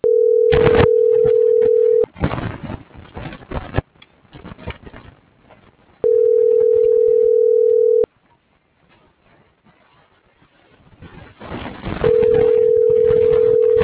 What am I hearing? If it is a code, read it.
Ringback tone (North America), 440 + 480 Hz.
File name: CENTRAL MABE / out david.toribio